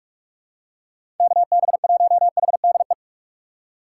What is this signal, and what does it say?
Morse code, heard as a keyed tone, about 42 wpm, 700 Hz. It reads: KB1HDE
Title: USA Call Signs - 6 Characters - 8x Character Spacing - 45wpm